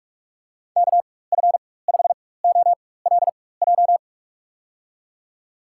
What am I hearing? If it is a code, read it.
Morse code, heard as a keyed tone, about 47 wpm, spaced as for 21 wpm, 700 Hz. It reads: KF5OLJ
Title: USA Call Signs - 6 Characters - 4x Character Spacing - 45wpm